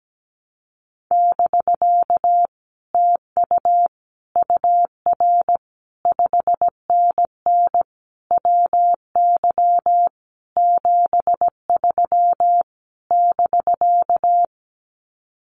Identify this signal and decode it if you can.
Morse code, heard as a keyed tone, about 17 wpm, 700 Hz. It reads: <BK> TU UR 5NN WY 73 <BK>
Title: Common QSO Phrases Encore 17wpm